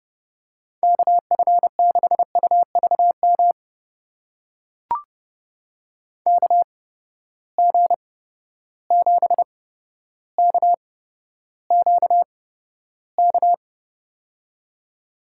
Morse code, heard as a keyed tone, about 32 wpm, spaced as for 13 wpm, 700 Hz. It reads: KF6UVM KG7KQK
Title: USA Call Signs - 6 Characters - 8x Character Spacing - 30wpm